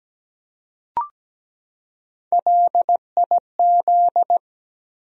Morse code, heard as a keyed tone, about 17 wpm, 700 Hz. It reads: LIZ
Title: USA Common Names - Max 3 Letters - 17wpm